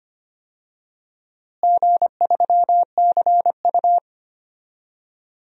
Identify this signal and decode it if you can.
Morse code, heard as a keyed tone, about 25 wpm, 700 Hz. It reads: G3CU
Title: DXCC-100 Call Signs - 4 Characters - Repeated 3x - 25wpm